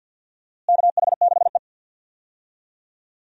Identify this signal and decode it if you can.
Morse code, heard as a keyed tone, about 46 wpm, 700 Hz. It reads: KH6E
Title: DXCC-100 Call Signs - 4 Characters - Repeated 2x - 50wpm